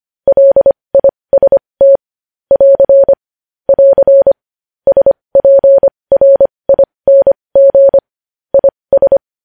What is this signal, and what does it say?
Morse code, heard as a keyed tone, about 25 wpm, spaced as for 20 wpm, 550 Hz. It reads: LIST <AR> <AR> SPRING IS
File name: podcast / News headlines-1x-25x20